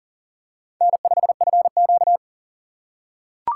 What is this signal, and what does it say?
Morse code, heard as a keyed tone, about 37 wpm, 700 Hz. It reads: N5FQ
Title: USA Call Signs - 4 Characters - 3x Character Spacing - 40wpm